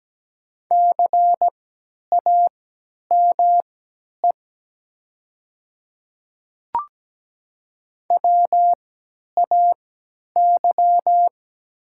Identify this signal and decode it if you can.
Morse code, heard as a keyed tone, about 17 wpm, spaced as for 10 wpm, 700 Hz. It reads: CAME WAY
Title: 3x Character Spacing - Top 200 Words - 17wpm